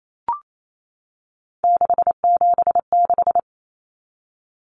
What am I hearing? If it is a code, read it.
Morse code, heard as a keyed tone, about 28 wpm, 700 Hz. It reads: 676
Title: Numbers - 3-Digit 28wpm